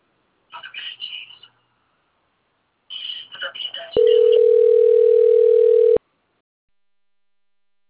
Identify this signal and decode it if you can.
Ringback tone (North America), 440 + 480 Hz, with speech.
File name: rg-1600-unknown- - - .51844